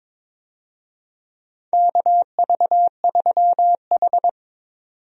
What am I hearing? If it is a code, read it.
Morse code, heard as a keyed tone, about 22 wpm, 700 Hz. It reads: KV3H
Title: USA Call Signs - 4 Characters - 22wpm